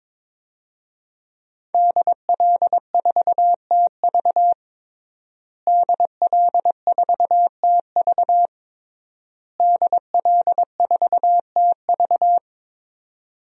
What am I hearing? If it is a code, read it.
Morse code, heard as a keyed tone, about 22 wpm, 700 Hz. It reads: DL4TV DL4TV DL4TV
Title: DXCC-100 Call Signs - 5 Characters - Repeated 3x - 22wpm